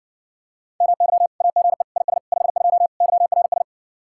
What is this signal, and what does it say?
Morse code, heard as a keyed tone, about 59 wpm, 700 Hz. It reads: MY AGE IS 52 YRS